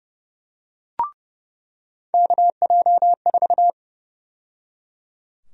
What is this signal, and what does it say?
Morse code, heard as a keyed tone, about 30 wpm, 700 Hz. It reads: KJ4
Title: USA Call Sign Prefixes - 30wpm